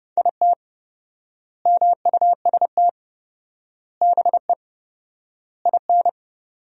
Morse code, heard as a keyed tone, about 30 wpm, 700 Hz. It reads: IT MUST BE IN
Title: 4x Word-Spacing - Sentences from Top 300 Words - 30wpm